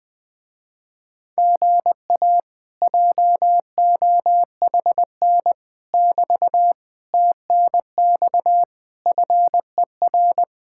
Morse code, heard as a keyed tone, about 20 wpm, 700 Hz. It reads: GA JOHN <BT> TNX FER
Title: Speed-Racing Common QSO Phrases - 20wpm